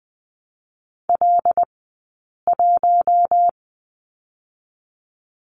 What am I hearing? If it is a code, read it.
Morse code, heard as a keyed tone, about 20 wpm, 700 Hz. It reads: L 1
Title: 158 Lesson 26d - Letter g - Two Characters 20wpm